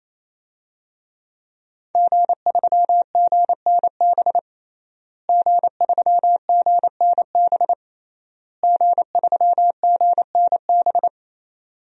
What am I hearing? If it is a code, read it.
Morse code, heard as a keyed tone, about 28 wpm, 700 Hz. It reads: G3GNB G3GNB G3GNB
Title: DXCC-100 Call Signs - 5 Characters - Repeated 3x - 28wpm